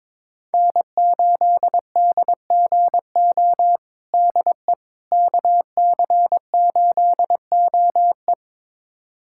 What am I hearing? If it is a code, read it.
Morse code, heard as a keyed tone, about 22 wpm, 700 Hz. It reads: N8DGO DE KC8OE
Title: USA Callsign DE Callsign - 5 Characters - 22wpm